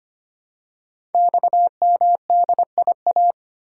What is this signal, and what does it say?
Morse code, heard as a keyed tone, about 25 wpm, 700 Hz. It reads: XMDIA
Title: Code Groups - 5 Letters - 25wpm